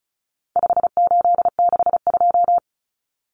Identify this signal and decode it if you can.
Morse code, heard as a keyed tone, about 34 wpm, 700 Hz. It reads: 5862
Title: Common Cut Numbers - 4-Digit 35wpm